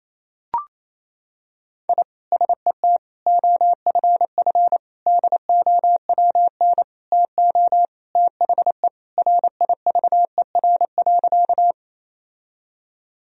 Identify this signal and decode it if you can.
Morse code, heard as a keyed tone, about 28 wpm, 700 Hz. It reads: I SET OFF DOWN TO THE RIVER.
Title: Sentences from Top 1000 Words - Review Part 1 - 28wpm